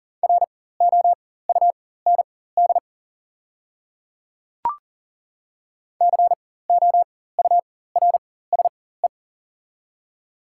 Morse code, heard as a keyed tone, about 40 wpm, spaced as for 19 wpm, 700 Hz. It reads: ROUND COURSE
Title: 4x Character Spacing - Top 500 Words - 40wpm